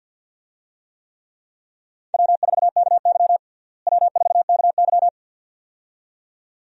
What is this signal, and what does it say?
Morse code, heard as a keyed tone, about 47 wpm, 700 Hz. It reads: W4KY W4KY
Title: USA Call Signs - 4 Characters - Repeated 2x - 50wpm